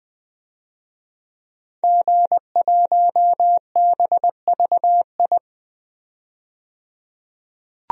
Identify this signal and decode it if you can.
Morse code, heard as a keyed tone, about 20 wpm, 700 Hz. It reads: G1BVI
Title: DXCC-100 Call Signs - 5 Characters - 20wpm